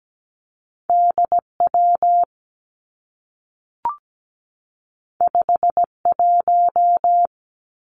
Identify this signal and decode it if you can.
Morse code, heard as a keyed tone, about 17 wpm, 700 Hz. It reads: D W 5 1
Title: Two Challenging Letter-Number Combinations 17wpm